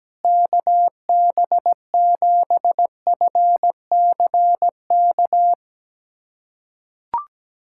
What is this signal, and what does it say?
Morse code, heard as a keyed tone, about 17 wpm, 700 Hz. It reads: KB7FCK
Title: USA Callsign DE Callsign - 6 Characters - 17wpm